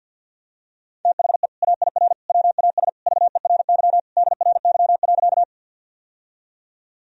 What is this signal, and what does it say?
Morse code, heard as a keyed tone, about 51 wpm, 700 Hz. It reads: THE AIR WAS VERY DRY.
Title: Sentences from Top 1000 Words - Review Part 3 - 50wpm